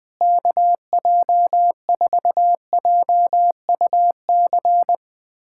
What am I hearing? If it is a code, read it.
Morse code, heard as a keyed tone, about 21 wpm, 700 Hz. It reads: KJ4JUC
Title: USA Callsign DE Callsign 20wpm